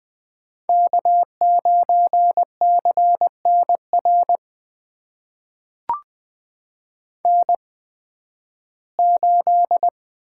Morse code, heard as a keyed tone, about 20 wpm, spaced as for 13 wpm, 700 Hz. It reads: K9CNR N8
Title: USA Call Signs - 5 Characters - 8x Character Spacing - 20wpm